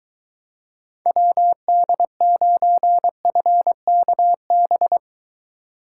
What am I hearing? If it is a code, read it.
Morse code, heard as a keyed tone, about 23 wpm, 700 Hz. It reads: WD9FKB